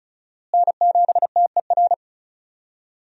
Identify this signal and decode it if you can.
Morse code, heard as a keyed tone, about 34 wpm, 700 Hz. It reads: N7TER
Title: USA Call Signs - 5 Characters - Repeated 3x - 35wpm